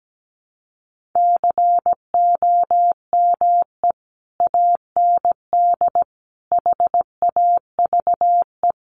Morse code, heard as a keyed tone, about 17 wpm, 700 Hz. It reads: COME AND HAVE
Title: Sentences from Top 1500 Words 17wpm